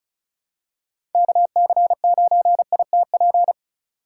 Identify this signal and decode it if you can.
Morse code, heard as a keyed tone, about 36 wpm, 700 Hz. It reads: KC9ITP
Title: USA Call Signs - 3x Character Spacing - 35wpm